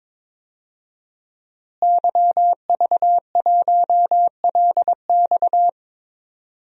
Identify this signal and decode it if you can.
Morse code, heard as a keyed tone, about 22 wpm, 700 Hz. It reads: YV1LX
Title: DXCC-100 Call Signs - 5 Characters - Repeated 2x - 22wpm